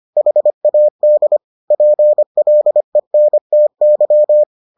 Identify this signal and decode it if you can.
Morse code, heard as a keyed tone, about 25 wpm, 600 Hz. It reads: HAD PLENTY